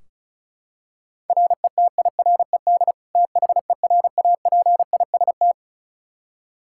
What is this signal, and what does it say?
Morse code, heard as a keyed tone, about 34 wpm, 700 Hz. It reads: RETIRED THERAPIST
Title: Practice QSO - Exchange 3 - More Info - 35wpm